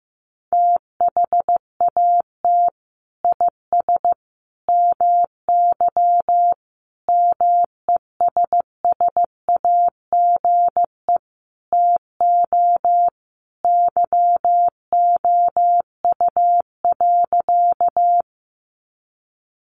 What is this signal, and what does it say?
Morse code, heard as a keyed tone, about 15 wpm, 700 Hz. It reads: THAT IS MY MESSAGE TO YOU.
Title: Sentences from Top 1100 Words 15wpm